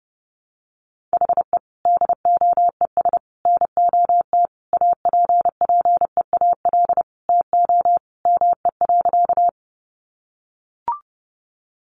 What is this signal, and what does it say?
Morse code, heard as a keyed tone, about 30 wpm, 700 Hz. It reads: HE DOES NOT APPEAL TO ME.
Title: Sentences from Top 1300 Words 30wpm